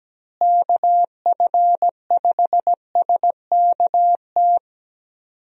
Morse code, heard as a keyed tone, about 17 wpm, 700 Hz. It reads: KF5SKT
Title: USA Call Signs - 6 Characters - Repeated 2x - 17wpm